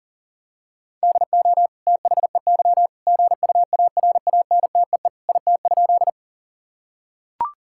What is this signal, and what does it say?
Morse code, heard as a keyed tone, about 37 wpm, 700 Hz. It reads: DO THEY GUARANTEE IT?
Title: Sentences from Top 1600 Words 40wpm